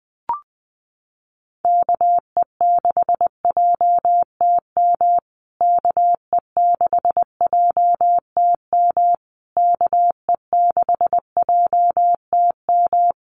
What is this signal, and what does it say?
Morse code, heard as a keyed tone, about 20 wpm, 700 Hz. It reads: KE6JTM KE6JTM KE6JTM
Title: USA Call Signs Repeated 3x 20wpm